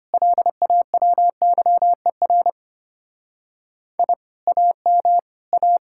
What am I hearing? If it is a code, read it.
Morse code, heard as a keyed tone, about 31 wpm, 700 Hz. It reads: LAWYER I AM A